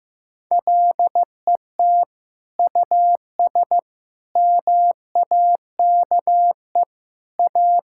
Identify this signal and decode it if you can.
Morse code, heard as a keyed tone, about 15 wpm, 700 Hz. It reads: LET US MAKE A
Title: Sentences from Top 1600 Words 15wpm